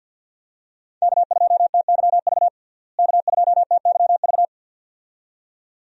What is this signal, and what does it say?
Morse code, heard as a keyed tone, about 52 wpm, 700 Hz. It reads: K2TYV K2TYV
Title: USA Call Signs - 5 Characters - Repeated 2x - 50wpm